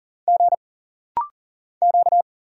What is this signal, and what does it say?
Morse code, heard as a keyed tone, about 40 wpm, 700 Hz. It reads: G Q
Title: Single Letters - Rapid-Fire 40wpm